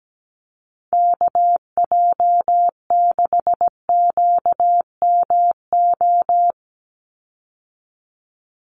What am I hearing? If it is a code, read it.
Morse code, heard as a keyed tone, about 17 wpm, 700 Hz. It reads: KJ6QMO
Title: USA Call Signs 17wpm